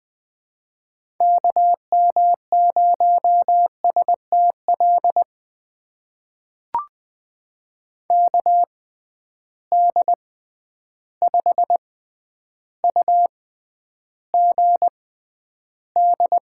Morse code, heard as a keyed tone, about 21 wpm, spaced as for 10 wpm, 700 Hz. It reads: KM0STL KD5UGD
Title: USA Call Signs - 6 Characters - 6x Character Spacing - 20wpm